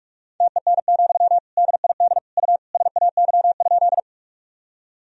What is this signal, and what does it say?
Morse code, heard as a keyed tone, about 47 wpm, 700 Hz. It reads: TEN, DID U SAY?
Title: Sentences from Top 400 Words and Abbreviations 45wpm